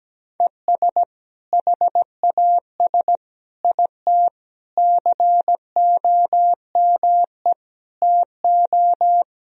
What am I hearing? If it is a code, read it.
Morse code, heard as a keyed tone, about 17 wpm, 700 Hz. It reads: ES HAS IT COME TO
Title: Sentences from Top 100 Words and Abbreviations 17wpm